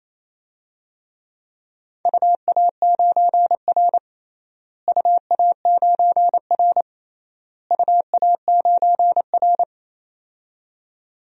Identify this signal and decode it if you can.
Morse code, heard as a keyed tone, about 28 wpm, 700 Hz. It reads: UA9R UA9R UA9R
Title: DXCC-100 Call Signs - 4 Characters - Repeated 3x - 28wpm